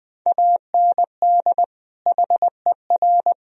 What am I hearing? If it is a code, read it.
Morse code, heard as a keyed tone, about 23 wpm, 700 Hz. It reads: AND HER
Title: Speed-Racing - Sets of 2 Words 20wpm